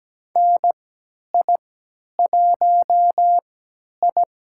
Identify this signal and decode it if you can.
Morse code, heard as a keyed tone, about 17 wpm, spaced as for 10 wpm, 700 Hz. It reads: NI1I
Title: USA Call Signs - 4 Characters - 3x Character Spacing - 17wpm